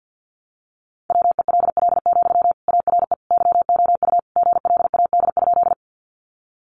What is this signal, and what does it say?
Morse code, heard as a keyed tone, about 50 wpm, 700 Hz. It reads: WELL, ARE YOU GLAD?
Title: Sentences from Top 800 Words 50wpm